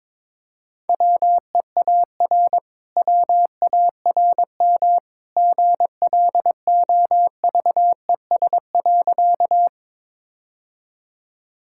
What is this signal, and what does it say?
Morse code, heard as a keyed tone, about 22 wpm, 690 Hz. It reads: WEAR WARM GLOVES.
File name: practice / 22wpm / Easy Sentences with Hints - 22wpm